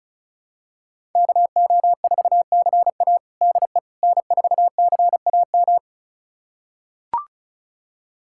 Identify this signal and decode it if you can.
Morse code, heard as a keyed tone, about 34 wpm, 700 Hz. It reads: KO4CA DE N4CAM
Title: USA Callsign DE Callsign - 5 Characters - 35wpm